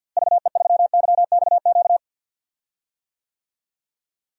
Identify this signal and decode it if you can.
Morse code, heard as a keyed tone, about 47 wpm, 700 Hz. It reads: VE3YXQ
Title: DXCC-100 Call Signs - 6 Characters - Repeated 3x - 50wpm